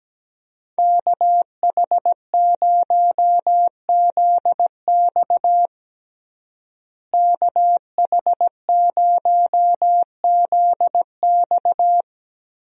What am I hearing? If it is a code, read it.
Morse code, heard as a keyed tone, about 17 wpm, 700 Hz. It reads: KH0ZX KH0ZX
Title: USA Call Signs - 5 Characters - Repeated 3x - 17wpm